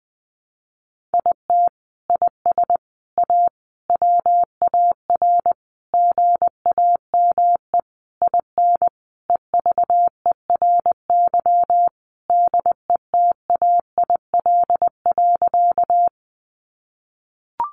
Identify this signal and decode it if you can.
Morse code, heard as a keyed tone, about 20 wpm, 700 Hz. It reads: IT IS A WAR GAME IN EVERY DETAIL.
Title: Sentences from Top 1500 Words 20wpm